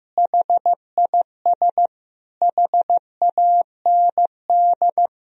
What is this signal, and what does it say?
Morse code, heard as a keyed tone, about 15 wpm, 700 Hz. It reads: HIS HAND
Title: Sentences from Top 1000 Words - Review Part 2 - 15wpm